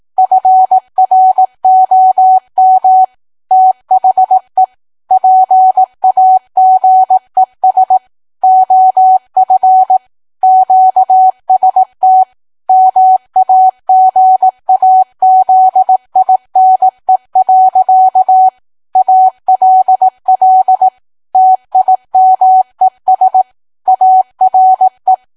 Morse code, heard as a keyed tone, about 18 wpm, 750 Hz. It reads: FROM THE PAGES OF QST MAGAZINE. ALL TIMES ARE